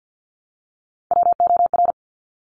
Morse code, heard as a keyed tone, about 50 wpm, 700 Hz. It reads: WOR
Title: Top Three Letter Combinations in Words 50wpm